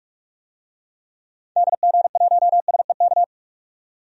Morse code, heard as a keyed tone, about 43 wpm, 700 Hz. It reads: DG1SEK
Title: DXCC-100 Call Signs - 6 Characters - 45wpm